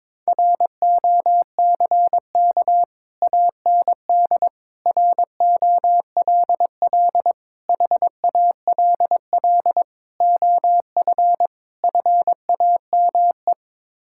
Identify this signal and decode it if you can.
Morse code, heard as a keyed tone, about 22 wpm, 700 Hz. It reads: ROCK AND ROLL HALL OF FAME
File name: practice / 22wpm / Sets of 6 Words - 22wpm